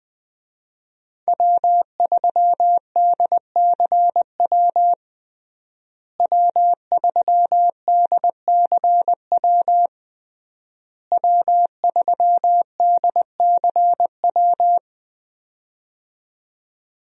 Morse code, heard as a keyed tone, about 20 wpm, 700 Hz. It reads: W3DCW W3DCW W3DCW
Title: USA Call Signs - 5 Characters - Repeated 3x - 20wpm